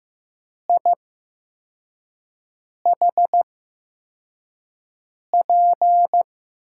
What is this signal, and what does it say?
Morse code, heard as a keyed tone, about 15 wpm, spaced as for 4 wpm, 700 Hz. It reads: IHP
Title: USA Call Signs - 5 Characters - 8x Character Spacing - 15wpm